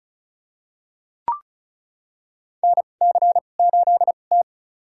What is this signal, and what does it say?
Morse code, heard as a keyed tone, about 36 wpm, spaced as for 25 wpm, 700 Hz. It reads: NC8T
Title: USA Call Signs - 4 Characters - 2x Character Spacing - 35wpm